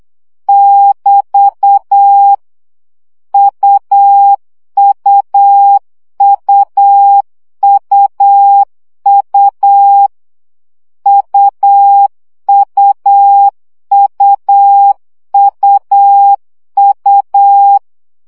Morse code, heard as a keyed tone, about 8 wpm, 800 Hz. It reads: <BT> UUUUU UUUUU
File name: CW gyakorlas u 7 baud